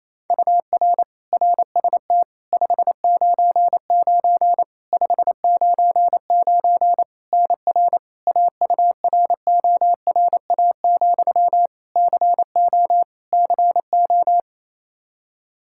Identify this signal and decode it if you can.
Morse code, heard as a keyed tone, about 27 wpm, 700 Hz. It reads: UR RST 599 599 NR AURORA, CO CO
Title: K1USN SST QSO Practice 28wpm